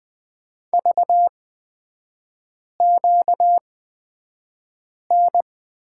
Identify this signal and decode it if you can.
Morse code, heard as a keyed tone, about 20 wpm, spaced as for 5 wpm, 700 Hz. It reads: VQN
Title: 236 Lesson 38e - Letter q - Three Characters 20wpm